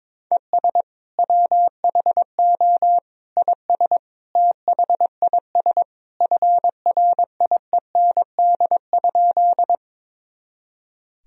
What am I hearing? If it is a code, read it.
Morse code, heard as a keyed tone, about 22 wpm, 700 Hz. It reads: ES WHO IS THIS FRIEND?
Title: Sentences from Top 300 Words and Abbreviations 22wpm